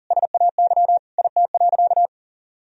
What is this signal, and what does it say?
Morse code, heard as a keyed tone, about 40 wpm, 700 Hz. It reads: SAY IT.
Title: Sentences from Top 400 Words and Abbreviations 40wpm